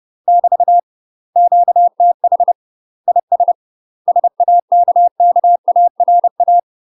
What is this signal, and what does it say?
Morse code, heard as a keyed tone, about 30 wpm, 700 Hz. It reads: <BT> QTH IS SAKKARA
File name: UK MASTER QSOs TRAINING / 30WPM / UK MASTER QSOs 0003